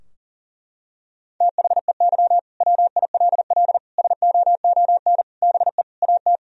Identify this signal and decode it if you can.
Morse code, heard as a keyed tone, about 40 wpm, 700 Hz. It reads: THEY WILL SOON BE AT
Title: Sentences from Top 1000 Words - Review Part 4 - 40wpm